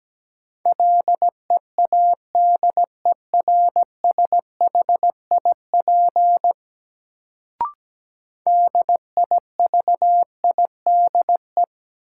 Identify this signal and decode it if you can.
Morse code, heard as a keyed tone, about 17 wpm, 700 Hz. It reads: LEADERSHIP DIVIDE
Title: Sentences from Top 1600 Words - New Word Drill Alt - 17wpm